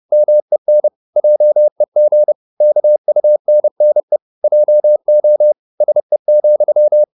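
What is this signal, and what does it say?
Morse code, heard as a keyed tone, about 30 wpm, 600 Hz. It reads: MEN JEG KUNNE JO SE,
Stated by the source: High-speed Morse (QRQ) at 30 wpm.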